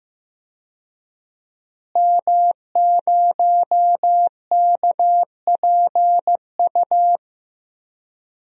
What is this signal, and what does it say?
Morse code, heard as a keyed tone, about 15 wpm, 700 Hz. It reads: M0KPU
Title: DXCC-100 Call Signs - 5 Characters - Repeated 3x - 15wpm